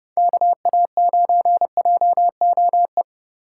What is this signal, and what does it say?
Morse code, heard as a keyed tone, about 32 wpm, 700 Hz. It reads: KA9JOE
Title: USA Call Signs - 4x Character Spacing - 30wpm